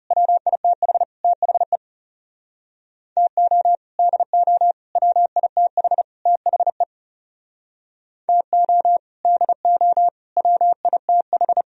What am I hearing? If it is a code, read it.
Morse code, heard as a keyed tone, about 37 wpm, 700 Hz. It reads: WITH THE TO DO WITH THE TO DO WITH
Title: Speed-Racing - Sets of 4 Words 30wpm